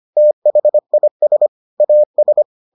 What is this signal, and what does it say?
Morse code, heard as a keyed tone, about 25 wpm, 600 Hz. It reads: THIS AS